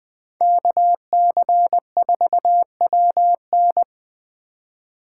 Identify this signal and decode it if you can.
Morse code, heard as a keyed tone, about 20 wpm, 700 Hz. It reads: KC4WN